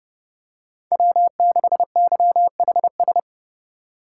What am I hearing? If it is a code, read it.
Morse code, heard as a keyed tone, about 29 wpm, 700 Hz. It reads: W6YHS
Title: USA Call Signs - 5 Characters - 2x Character Spacing - 30wpm